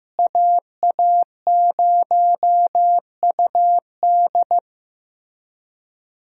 Morse code, heard as a keyed tone, about 15 wpm, 700 Hz. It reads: AA0UD
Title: USA Call Signs - 5 Characters - Repeated 3x - 15wpm